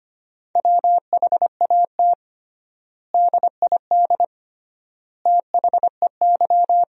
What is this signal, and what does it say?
Morse code, heard as a keyed tone, about 25 wpm, 700 Hz. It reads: WHAT DID THEY
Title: 3x Word-Spacing - Sentences from Top 100 Words - 25wpm